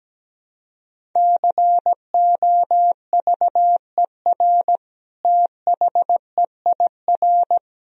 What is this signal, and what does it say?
Morse code, heard as a keyed tone, about 17 wpm, 700 Hz. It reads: COVER THEIR
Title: Sentences from Top 1000 Words - Review Part 1 - 17wpm